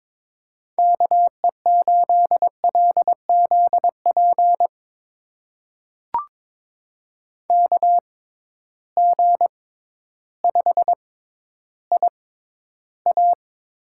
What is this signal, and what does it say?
Morse code, heard as a keyed tone, about 22 wpm, spaced as for 12 wpm, 700 Hz. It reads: KE8LZP KG5IA
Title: USA Call Signs - 6 Characters - 6x Character Spacing - 22wpm